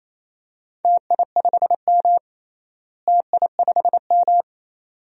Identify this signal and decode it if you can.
Morse code, heard as a keyed tone, about 28 wpm, 700 Hz. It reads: TI5M TI5M
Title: DXCC-100 Call Signs - 4 Characters - Repeated 2x - 28wpm